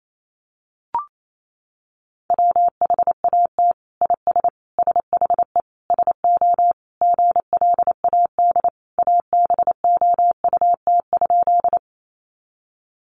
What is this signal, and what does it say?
Morse code, heard as a keyed tone, about 28 wpm, 700 Hz. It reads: WHAT IS SHE SO GLAD ABOUT?
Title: Sentences from Top 800 Words 28wpm